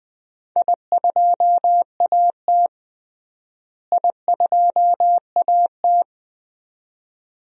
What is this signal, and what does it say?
Morse code, heard as a keyed tone, about 21 wpm, 700 Hz. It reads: I2AT I2AT
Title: DXCC-100 Call Signs - 4 Characters - Repeated 2x - 20wpm